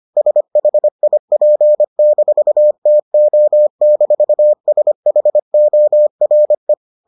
Morse code, heard as a keyed tone, about 25 wpm, 600 Hz. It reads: SHIP-TO-SHORE